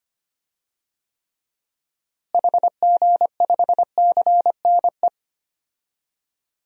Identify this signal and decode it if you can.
Morse code, heard as a keyed tone, about 25 wpm, 700 Hz. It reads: HG5CNE
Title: DXCC-100 Call Signs - 6 Characters - Repeated 2x - 25wpm